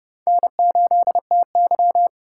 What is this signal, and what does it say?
Morse code, heard as a keyed tone, about 31 wpm, 700 Hz. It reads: N8TY